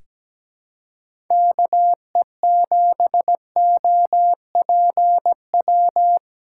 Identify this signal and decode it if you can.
Morse code, heard as a keyed tone, about 17 wpm, 700 Hz. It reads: KE7OPW
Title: USA Callsign DE Callsign - 6 Characters - 17wpm